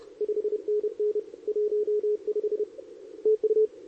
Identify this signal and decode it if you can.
Morse code, heard as a keyed tone, about 31 wpm, 440 Hz. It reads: 5NN 15 TU
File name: CQWWCW2023 / SP